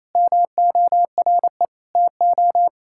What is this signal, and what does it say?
Morse code, heard as a keyed tone, about 28 wpm, 700 Hz. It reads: MORE TO
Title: Sentences from Top 1000 Words - Review Part 5 - 28wpm